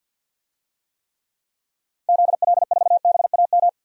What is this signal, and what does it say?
Morse code, heard as a keyed tone, about 48 wpm, 700 Hz. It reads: ZL4BAM
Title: DXCC-100 Call Signs - 6 Characters - 50wpm